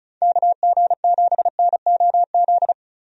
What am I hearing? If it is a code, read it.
Morse code, heard as a keyed tone, about 36 wpm, 700 Hz. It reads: KG7NOZ
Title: USA Call Signs - 6 Characters - 35wpm